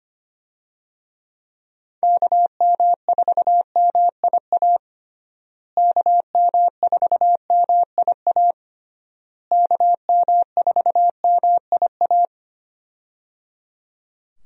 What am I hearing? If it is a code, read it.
Morse code, heard as a keyed tone, about 25 wpm, 700 Hz. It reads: KM4MIA KM4MIA KM4MIA
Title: USA Call Signs - 6 Characters - Repeated 3x - 25wpm